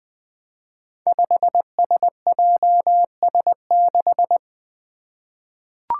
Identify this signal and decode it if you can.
Morse code, heard as a keyed tone, about 20 wpm, 700 Hz. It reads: 5SJS6
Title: Five Challenging Letter-Number Combinations 20wpm